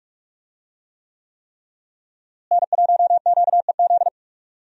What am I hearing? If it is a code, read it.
Morse code, heard as a keyed tone, about 46 wpm, 700 Hz. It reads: N1QEZ
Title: USA Call Signs - 5 Characters - Repeated 3x - 45wpm